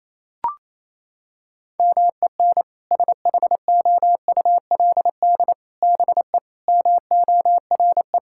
Morse code, heard as a keyed tone, about 28 wpm, 700 Hz. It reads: MEN SHOULD BE MORE
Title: Sentences from Top 1600 Words 28wpm